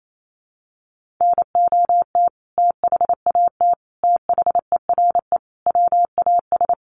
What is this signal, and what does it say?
Morse code, heard as a keyed tone, about 28 wpm, 700 Hz. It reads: NOT THAT THERE WAS
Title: Sentences from Top 1400 Words 28wpm